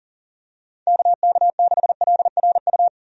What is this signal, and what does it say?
Morse code, heard as a keyed tone, about 40 wpm, 700 Hz. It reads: KK6LRU
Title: USA Call Signs - 6 Characters - 40wpm